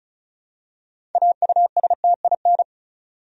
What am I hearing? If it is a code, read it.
Morse code, heard as a keyed tone, about 36 wpm, 700 Hz. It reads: AUSTIN
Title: USA Common Names - 35wpm